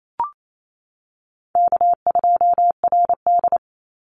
Morse code, heard as a keyed tone, about 28 wpm, 700 Hz. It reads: K2RD
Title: Call Signs - 28wpm